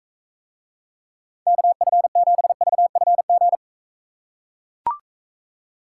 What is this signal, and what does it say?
Morse code, heard as a keyed tone, about 43 wpm, 700 Hz. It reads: KF7VFG